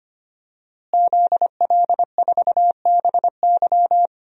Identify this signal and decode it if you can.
Morse code, heard as a keyed tone, about 25 wpm, 700 Hz. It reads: ZL4BY
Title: DXCC-100 Call Signs - 5 Characters - Repeated 2x - 25wpm